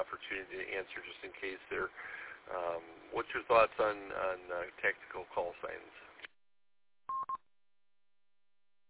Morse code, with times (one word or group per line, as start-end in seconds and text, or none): none